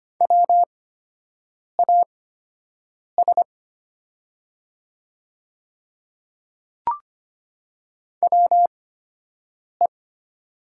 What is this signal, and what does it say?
Morse code, heard as a keyed tone, about 25 wpm, spaced as for 7 wpm, 700 Hz. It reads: WAS WE
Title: 8x Character Spacing - Top 100 Words - 25wpm